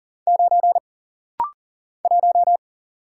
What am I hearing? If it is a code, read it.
Morse code, heard as a keyed tone, about 40 wpm, 700 Hz. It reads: 9 1
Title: Single Letter-Number Alt - Rapid-Fire 40wpm